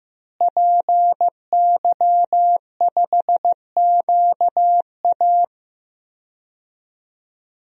Morse code, heard as a keyed tone, about 15 wpm, 700 Hz. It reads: PY5QA
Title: DXCC-100 Call Signs - 5 Characters - Repeated 3x - 15wpm